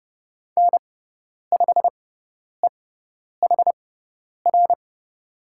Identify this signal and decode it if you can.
Morse code, heard as a keyed tone, about 30 wpm, spaced as for 10 wpm, 700 Hz. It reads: N5EHR
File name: Lesson 17g - Letter u - Call Signs 30wpm